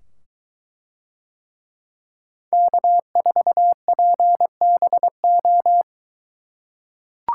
Morse code, heard as a keyed tone, about 22 wpm, 700 Hz. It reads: K4PBO